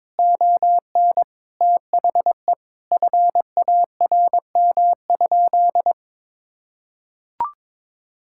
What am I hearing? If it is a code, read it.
Morse code, heard as a keyed tone, about 23 wpm, 700 Hz. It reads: ON THE FARM?
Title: Sentences from Top 1000 Words - Review Part 6 - 22wpm